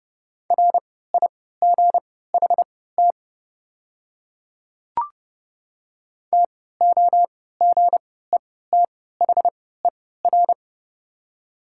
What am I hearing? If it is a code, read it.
Morse code, heard as a keyed tone, about 32 wpm, spaced as for 17 wpm, 700 Hz. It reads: RIGHT TOGETHER
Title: 3x Character Spacing - Top 300 Words - 30wpm